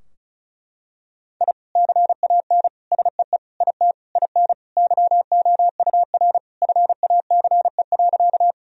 Morse code, heard as a keyed tone, about 36 wpm, 700 Hz. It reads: I CAN SEE IT IN YOUR FACE.
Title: Sentences from Top 1000 Words - Review Part 2 - 35wpm